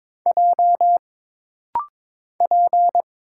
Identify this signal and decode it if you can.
Morse code, heard as a keyed tone, about 22 wpm, 700 Hz. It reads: J P
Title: Single Letter-Number Alt - Rapid-Fire 22wpm